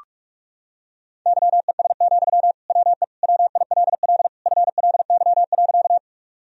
Morse code, heard as a keyed tone, about 43 wpm, 700 Hz. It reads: YES, WE WILL FLY.
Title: Sentences from Top 1000 Words - Review Part 2 - 45wpm